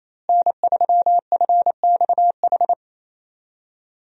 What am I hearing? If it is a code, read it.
Morse code, heard as a keyed tone, about 28 wpm, 700 Hz. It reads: N3FXH
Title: USA Call Signs - 5 Characters - Repeated 2x - 28wpm